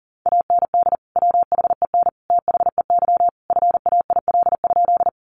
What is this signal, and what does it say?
Morse code, heard as a keyed tone, about 40 wpm, 700 Hz. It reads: AND WHEN THEY FAIL?
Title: Sentences from Top 1100 Words 40wpm